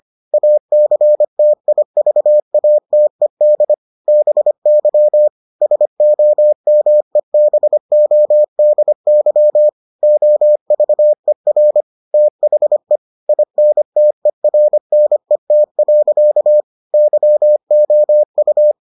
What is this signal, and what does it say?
Morse code, heard as a keyed tone, about 25 wpm, 600 Hz. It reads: ACTIVATED BY SOMEBODY OVER THE INTERNET. YOU